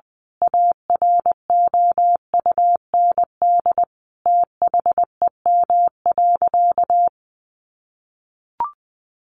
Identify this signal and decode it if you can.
Morse code, heard as a keyed tone, about 20 wpm, 700 Hz. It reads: AROUND THEM.
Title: A Sound Of Thunder 20wpm